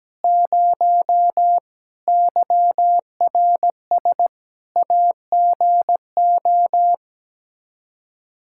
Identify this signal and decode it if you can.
Morse code, heard as a keyed tone, about 17 wpm, 700 Hz. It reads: 0 YRS AGO
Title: Practice QSO - Exchange 3 - More Info - 17wpm